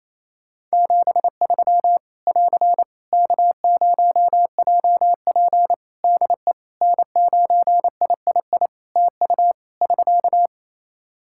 Morse code, heard as a keyed tone, about 28 wpm, 700 Hz. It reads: 73 <AR> K0JP DE N9III TU <SK>
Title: Practice QSO - Exchange 4 - Ending - 28wpm